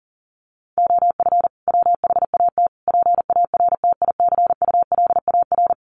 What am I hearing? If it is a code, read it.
Morse code, heard as a keyed tone, about 40 wpm, 700 Hz. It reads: OF WHAT PARTICULAR
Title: Sentences from Top 1300 Words 40wpm